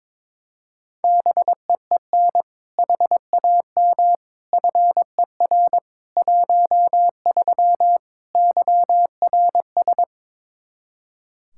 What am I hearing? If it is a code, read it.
Morse code, heard as a keyed tone, about 22 wpm, 700 Hz. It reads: BEEN HAM FER 13 YRS
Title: Practice QSO - Exchange 3 - More Info - 22wpm